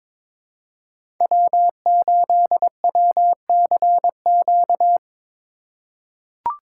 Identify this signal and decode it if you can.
Morse code, heard as a keyed tone, about 22 wpm, 700 Hz. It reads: W8WCQ
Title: USA Call Signs - 5 Characters - 4x Character Spacing - 22wpm